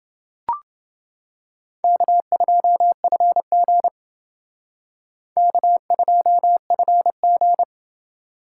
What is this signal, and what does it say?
Morse code, heard as a keyed tone, about 29 wpm, 700 Hz. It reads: K2FG K2FG